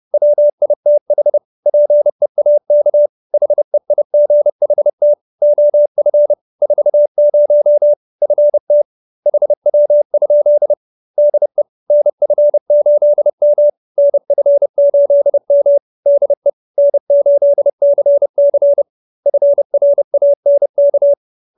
Morse code, heard as a keyed tone, about 30 wpm, 590 Hz. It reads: WITH PEAK HEIGHT OF 40 FT HW? DE NF8M NF8M DE N8CC FRANK